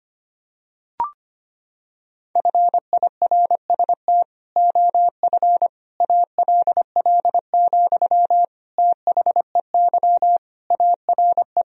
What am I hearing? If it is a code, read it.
Morse code, heard as a keyed tone, about 25 wpm, 700 Hz. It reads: FIRST OF ALL, THEY ARE
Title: Sentences from Top 1000 Words - Review Part 4 - 25wpm